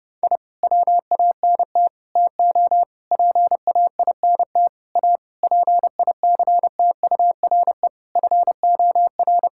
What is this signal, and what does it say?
Morse code, heard as a keyed tone, about 32 wpm, 700 Hz. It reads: I WANT TO PAINT A PICTURE FOR
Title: Sentences from Top 1000 Words - Review Part 6 - 30wpm